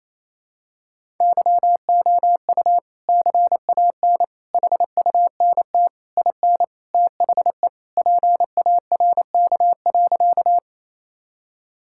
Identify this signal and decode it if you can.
Morse code, heard as a keyed tone, about 27 wpm, 700 Hz. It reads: YOU CAN HUNT IN THE PARK.
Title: Sentences from Top 1600 Words 28wpm